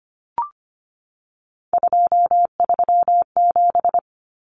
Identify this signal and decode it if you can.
Morse code, heard as a keyed tone, about 25 wpm, 700 Hz. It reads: 237
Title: Numbers - 3-Digit 25wpm